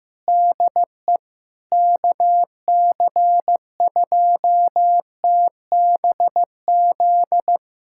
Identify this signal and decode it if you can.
Morse code, heard as a keyed tone, about 15 wpm, 700 Hz. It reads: DE KC2TBZ
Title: USA Callsign DE Callsign - 6 Characters - 15wpm